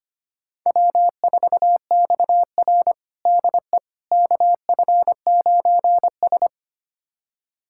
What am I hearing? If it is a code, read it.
Morse code, heard as a keyed tone, about 25 wpm, 700 Hz. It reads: W4XR DE KF9S
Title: USA Callsign DE Callsign - 4 Characters - 25wpm